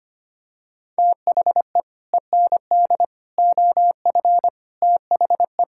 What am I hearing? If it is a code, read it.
Morse code, heard as a keyed tone, about 25 wpm, 700 Hz. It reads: THE END OF THE
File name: Sets of 9 Words - 25wpm